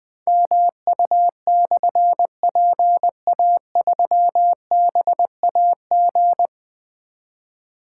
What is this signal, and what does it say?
Morse code, heard as a keyed tone, about 20 wpm, 700 Hz. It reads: MU/PA3BAG
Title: Difficult Call Signs 20wpm